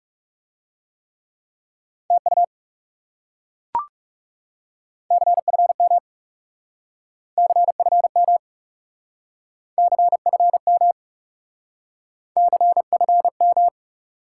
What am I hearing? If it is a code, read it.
Morse code, heard as a keyed tone, about 39 wpm, 700 Hz. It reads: TU CFM CFM CFM CFM